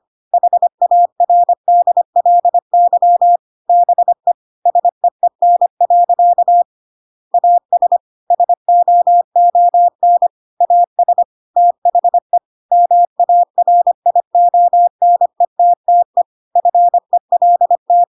Morse code, heard as a keyed tone, about 25 wpm, 700 Hz. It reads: HARDLY BE SEEN. AS SOON AS THE MARIONETTE FELT